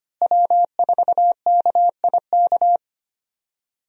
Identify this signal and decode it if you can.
Morse code, heard as a keyed tone, about 25 wpm, 700 Hz. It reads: W4KIK
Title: USA Call Signs - 5 Characters - Repeated 2x - 25wpm